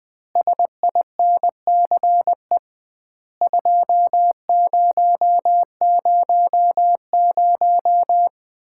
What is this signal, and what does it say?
Morse code, heard as a keyed tone, about 20 wpm, 700 Hz. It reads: SINCE 2000
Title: Common QSO Phrases - 2x Word-Spacing - 20wpm